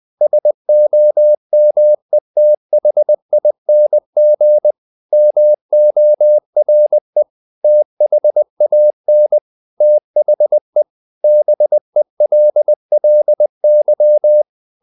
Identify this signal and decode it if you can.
Morse code, heard as a keyed tone, about 20 wpm, 600 Hz. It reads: SOMETHING MORE THAN THE BELLY
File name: the old man and the sea / the old man and the sea part 1 20wpm